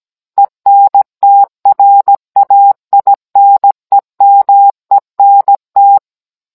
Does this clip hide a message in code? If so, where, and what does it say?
Yes: Morse code from 0.4 to 6.0 s, heard as a keyed tone, about 17 wpm, 800 Hz. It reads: ENTRAINEMENT